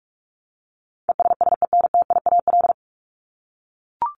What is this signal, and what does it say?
Morse code, heard as a keyed tone, about 44 wpm, 700 Hz. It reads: ESSENTIAL
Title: LY Suffix - Top 100 Words with Base Words 45wpm